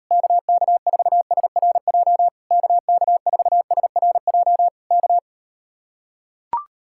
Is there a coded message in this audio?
Yes: Morse code from 0.1 to 5.2 s, heard as a keyed tone, about 39 wpm, 700 Hz. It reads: KK4SRJ KK4SRJ K